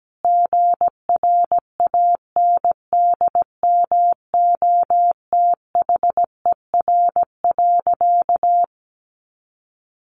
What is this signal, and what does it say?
Morse code, heard as a keyed tone, about 17 wpm, 700 Hz. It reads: GRANDMOTHER.
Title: Sentences from Top 1200 Words 17wpm